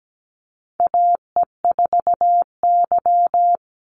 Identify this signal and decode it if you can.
Morse code, heard as a keyed tone, about 17 wpm, 700 Hz. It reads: AE4Y
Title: NA SOTA Chaser Callsigns 17wpm